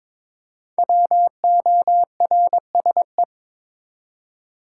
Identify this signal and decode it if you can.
Morse code, heard as a keyed tone, about 22 wpm, 700 Hz. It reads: WORSE
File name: 5 Letter Words - First 3 Letters Given - 22wpm